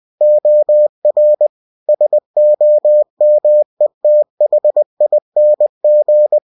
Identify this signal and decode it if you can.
Morse code, heard as a keyed tone, about 20 wpm, 600 Hz. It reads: OR SOMETHING
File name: cw-books / winnie-the-pooh / winnie-20wpm-page-0034